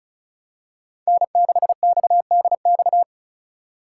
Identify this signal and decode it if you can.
Morse code, heard as a keyed tone, about 34 wpm, 700 Hz. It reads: N6XDX
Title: USA Call Signs - 5 Characters - 4x Character Spacing - 35wpm